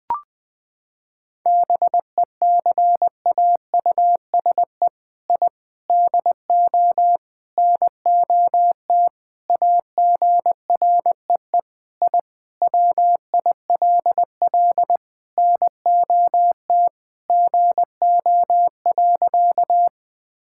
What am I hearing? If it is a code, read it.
Morse code, heard as a keyed tone, about 20 wpm, 700 Hz. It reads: BECAUSE I DO NOT AGREE I WILL NOT GO.
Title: Sentences with 4-Word Phrases - 20wpm